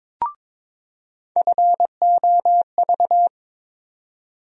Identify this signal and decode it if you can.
Morse code, heard as a keyed tone, about 22 wpm, 700 Hz. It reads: FOV
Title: USA Call Sign Postfixes - 3 Letters 22wpm